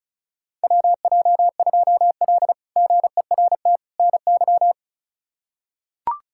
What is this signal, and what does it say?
Morse code, heard as a keyed tone, about 34 wpm, 700 Hz. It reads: WJ2L GERT NY
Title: K1USN SST QSO Practice - 35wpm